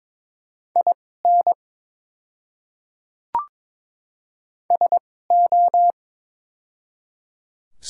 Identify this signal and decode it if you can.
Morse code, heard as a keyed tone, about 22 wpm, spaced as for 16 wpm, 700 Hz. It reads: IN SO
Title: 2x Character Spacing - Top 100 Words - 22wpm